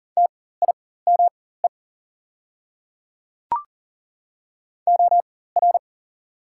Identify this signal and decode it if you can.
Morse code, heard as a keyed tone, about 40 wpm, spaced as for 19 wpm, 690 Hz. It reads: TIME OR
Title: 4x Character Spacing - Top 100 Words - 40wpm